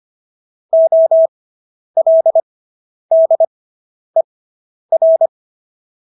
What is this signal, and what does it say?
Morse code, heard as a keyed tone, about 25 wpm, spaced as for 10 wpm, 650 Hz. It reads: OLDER